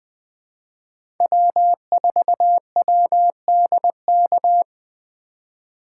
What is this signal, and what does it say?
Morse code, heard as a keyed tone, about 20 wpm, 700 Hz. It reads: W4WDK
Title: 2023 POTA Hunter Call Signs - 20wpm